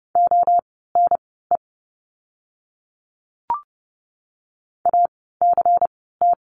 Morse code, heard as a keyed tone, about 30 wpm, spaced as for 17 wpm, 700 Hz. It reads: ONE ACT
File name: Lesson 16f - Letter c - Words 30wpm